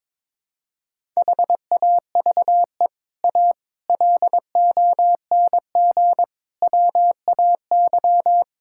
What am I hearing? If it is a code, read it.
Morse code, heard as a keyed tone, about 21 wpm, 700 Hz. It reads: HAVE A LONG WAY
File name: practice / 22wpm / Sets of 6 Words - 22wpm